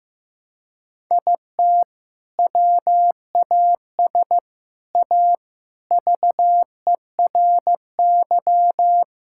Morse code, heard as a keyed tone, about 15 wpm, 700 Hz. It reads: IT WAS A VERY
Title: Sentences from Top 1000 Words - Review Part 6 - 15wpm